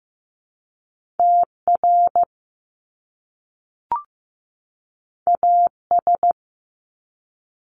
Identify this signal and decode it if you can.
Morse code, heard as a keyed tone, about 15 wpm, 700 Hz. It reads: TR AS
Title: Top Two Letter Combinations in Words 15wpm